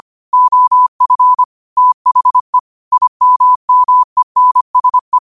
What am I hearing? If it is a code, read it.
Morse code, heard as a keyed tone, about 25 wpm, 1000 Hz. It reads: OF THE IMMENSE
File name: qotdfiles / 25wpm